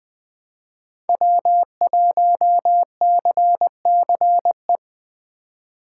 Morse code, heard as a keyed tone, about 20 wpm, 700 Hz. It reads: W1CCE
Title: K1USN SST QSO Practice - 20wpm